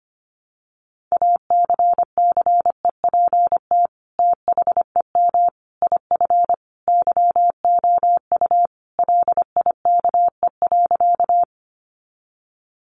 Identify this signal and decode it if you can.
Morse code, heard as a keyed tone, about 25 wpm, 700 Hz. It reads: ACCEPT THEM IF YOU LIKE.
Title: Sentences from Top 1300 Words 25wpm